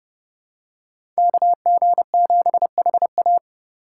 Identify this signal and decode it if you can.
Morse code, heard as a keyed tone, about 30 wpm, 700 Hz. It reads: KG7HA
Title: USA Call Signs - 4x Character Spacing - 30wpm